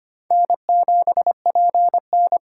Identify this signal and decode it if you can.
Morse code, heard as a keyed tone, about 25 wpm, 700 Hz. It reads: N7PN
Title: USA Call Signs - 4 Characters - 6x Character Spacing - 25wpm